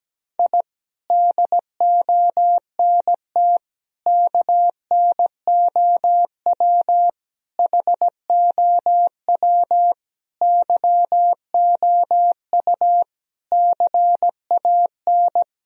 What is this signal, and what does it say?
Morse code, heard as a keyed tone, about 17 wpm, 700 Hz. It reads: I DONT KNOW HOW YOU CAN
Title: Sentences from Top 1000 Words - Review Part 2 - 17wpm